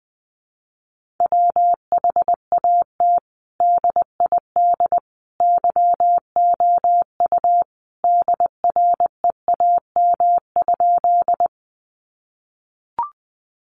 Morse code, heard as a keyed tone, about 20 wpm, 700 Hz. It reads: WHAT DID YOU DREAM?
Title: Sentences from Top 1500 Words 20wpm